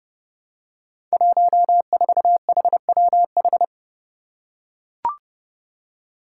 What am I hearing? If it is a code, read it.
Morse code, heard as a keyed tone, about 28 wpm, 700 Hz. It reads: 14HWH
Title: Five Challenging Letter-Number Combinations 30wpm